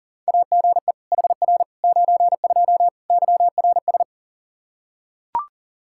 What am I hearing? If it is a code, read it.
Morse code, heard as a keyed tone, about 37 wpm, 700 Hz. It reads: AGE HR 92 YRS
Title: Practice QSO - Exchange 3 - More Info - 40wpm